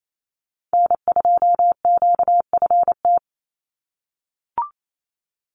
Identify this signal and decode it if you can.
Morse code, heard as a keyed tone, about 28 wpm, 700 Hz. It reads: N2QFT
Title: USA Call Signs 28wpm